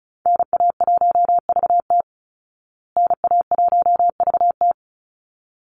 Morse code, heard as a keyed tone, about 34 wpm, 700 Hz. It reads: NA1VT NA1VT
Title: Call Signs Repeated 3x - 35wpm